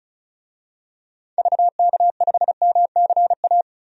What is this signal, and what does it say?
Morse code, heard as a keyed tone, about 34 wpm, 700 Hz. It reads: VK5MCA
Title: DXCC-100 Call Signs - 6 Characters - 35wpm